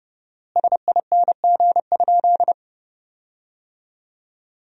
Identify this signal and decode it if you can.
Morse code, heard as a keyed tone, about 30 wpm, 700 Hz. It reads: SING?
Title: 4x Word-Spacing - Sentences from Top 400 Words - 30wpm